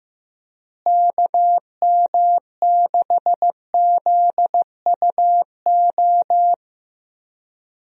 Morse code, heard as a keyed tone, about 15 wpm, 700 Hz. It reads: KM6ZUO
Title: USA Call Signs - 6 Characters - Repeated 2x - 15wpm